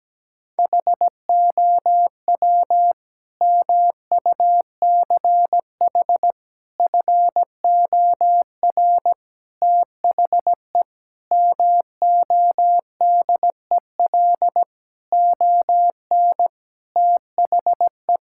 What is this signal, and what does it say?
Morse code, heard as a keyed tone, about 17 wpm, 700 Hz. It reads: HOW MUCH FOR THE MODEL ON THE
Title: Sentences from Top 1000 Words - Review Part 5 - 17wpm